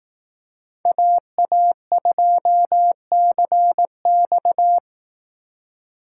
Morse code, heard as a keyed tone, about 18 wpm, 700 Hz. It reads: AA2CX
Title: Speed Racing - USA 2x2 Call Signs 15wpm